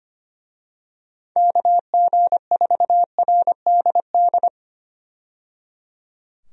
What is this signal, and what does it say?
Morse code, heard as a keyed tone, about 25 wpm, 700 Hz. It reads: KG4RDD
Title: USA Call Signs - 6 Characters - 25wpm